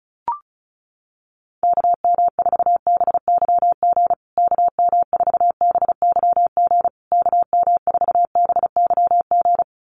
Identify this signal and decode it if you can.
Morse code, heard as a keyed tone, about 34 wpm, 700 Hz. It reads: KM4BYG KM4BYG KM4BYG
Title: USA Call Signs Repeated 3x 35wpm